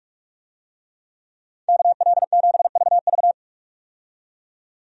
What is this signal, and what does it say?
Morse code, heard as a keyed tone, about 42 wpm, 700 Hz. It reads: KL7VV
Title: USA Call Signs - 5 Characters - 45wpm